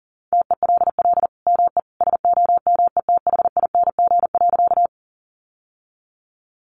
Morse code, heard as a keyed tone, about 40 wpm, 700 Hz. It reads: TELL ME SOMETHING.
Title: Sentences from Top 600 Words 40wpm